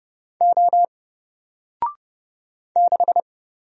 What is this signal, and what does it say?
Morse code, heard as a keyed tone, about 30 wpm, 700 Hz. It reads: O 6
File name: Single-Letter-Number-30wpm